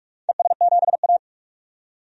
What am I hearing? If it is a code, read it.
Morse code, heard as a keyed tone, about 44 wpm, 700 Hz. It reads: ES7A
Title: DXCC-100 Call Signs - 4 Characters - 45wpm